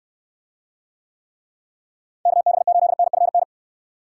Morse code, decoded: KB8NFN

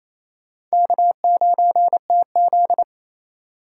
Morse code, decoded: K9TZ

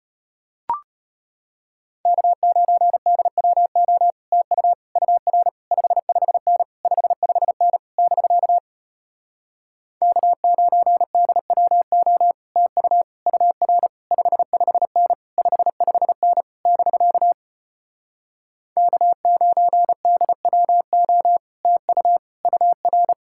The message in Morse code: K9DWO TU UR 55N 55N <BK> K9DWO TU UR 55N 55N <BK> K9DWO TU UR

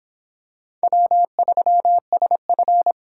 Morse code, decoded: W3SF

